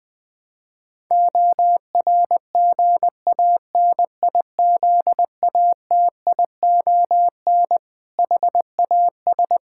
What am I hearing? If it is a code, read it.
Morse code, heard as a keyed tone, about 20 wpm, 700 Hz. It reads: ORGANIZATION HAS
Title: Sentences from Top 1000 Words - Review Part 5 - 20wpm